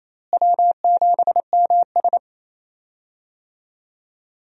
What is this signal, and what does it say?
Morse code, heard as a keyed tone, about 28 wpm, 700 Hz. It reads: W7MS
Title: USA Call Signs - 4 Characters - Repeated 3x - 28wpm